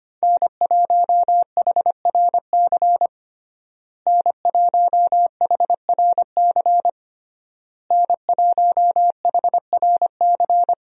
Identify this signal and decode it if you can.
Morse code, heard as a keyed tone, about 25 wpm, 700 Hz. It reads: N1HRC N1HRC N1HRC
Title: USA Call Signs - 5 Characters - Repeated 3x - 25wpm